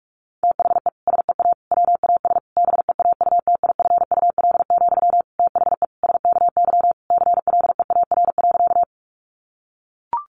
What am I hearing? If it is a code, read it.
Morse code, heard as a keyed tone, about 44 wpm, 700 Hz. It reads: THE SEA WAS BEAUTIFUL, THE SKY CLEAR.
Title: Sentences from Top 1200 Words 45wpm